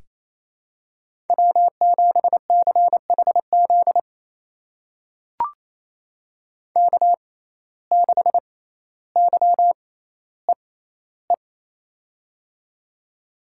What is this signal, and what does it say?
Morse code, heard as a keyed tone, about 28 wpm, spaced as for 14 wpm, 700 Hz. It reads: W7CHZ K6YEE
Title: USA Call Signs - 5 Characters - 6x Character Spacing - 28wpm